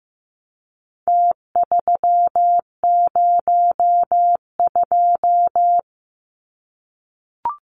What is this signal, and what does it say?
Morse code, heard as a keyed tone, about 15 wpm, 700 Hz. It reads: T302
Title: Numbers - Contest Style - 4-Digit 15wpm